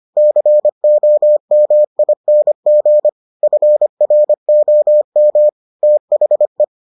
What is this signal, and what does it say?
Morse code, heard as a keyed tone, about 25 wpm, 600 Hz. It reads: COMING FROM THE